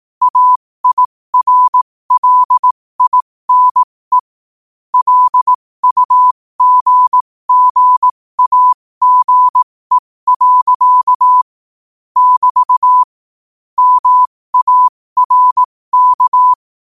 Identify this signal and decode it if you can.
Morse code, heard as a keyed tone, about 18 wpm, spaced as for 15 wpm, 1000 Hz. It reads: AIRLINE LUGGAGE. <BT> MARK